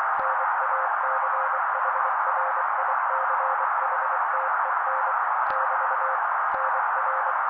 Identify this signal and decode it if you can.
Morse code, heard as a keyed tone, about 24 wpm, 530 Hz. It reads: NACHRICHTEN <BT> NR
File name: daily / 23wpm noise